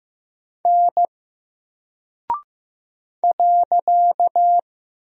Morse code, heard as a keyed tone, about 15 wpm, 700 Hz. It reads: N .
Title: Single Character (Letter:Number:Punctuation:Prosign) Alt 15wpm